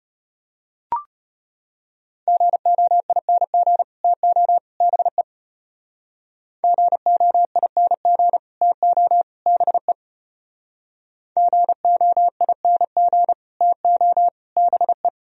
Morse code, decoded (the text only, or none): GOING TO BE GOING TO BE GOING TO BE